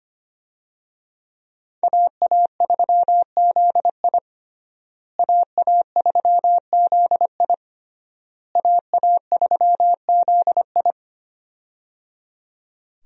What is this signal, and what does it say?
Morse code, heard as a keyed tone, about 25 wpm, 700 Hz. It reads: AA3ZI AA3ZI AA3ZI
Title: USA Call Signs - 5 Characters - Repeated 3x - 25wpm